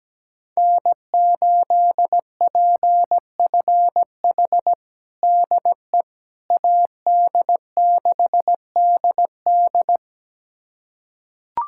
Morse code, heard as a keyed tone, about 17 wpm, 700 Hz. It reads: N8PFH DE AD6DD
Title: USA Callsign DE Callsign - 5 Characters - 17wpm